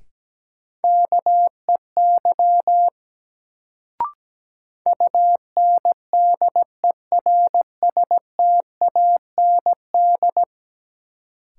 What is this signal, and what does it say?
Morse code, heard as a keyed tone, about 17 wpm, 700 Hz. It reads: KEY UNDERSTAND
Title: Sentences from Top 600 Words - New Word Drill - 17wpm